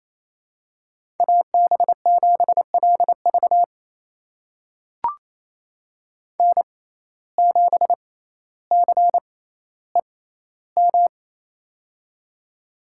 Morse code, heard as a keyed tone, about 28 wpm, spaced as for 14 wpm, 700 Hz. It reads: AB7LV N7CEM
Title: USA Call Signs - 5 Characters - 6x Character Spacing - 28wpm